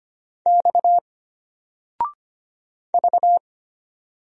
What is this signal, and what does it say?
Morse code, heard as a keyed tone, about 25 wpm, 700 Hz. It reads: X V
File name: Single-Letter-Number-25wpm